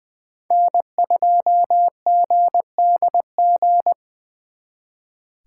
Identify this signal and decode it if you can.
Morse code, heard as a keyed tone, about 23 wpm, 700 Hz. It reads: N2GDG
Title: Speed Racing - USA 1x3 Call Signs 20wpm